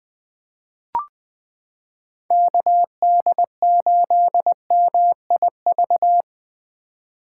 Morse code, heard as a keyed tone, about 20 wpm, 700 Hz. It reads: KD8MIV
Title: 2023 POTA Hunter Call Signs - 20wpm